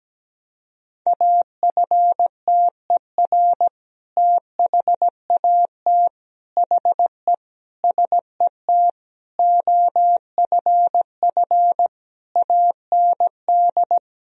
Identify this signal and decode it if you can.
Morse code, heard as a keyed tone, about 17 wpm, 700 Hz. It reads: AFTER THAT HE SET OFF AND
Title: Sentences from Top 1000 Words - Review Part 1 - 17wpm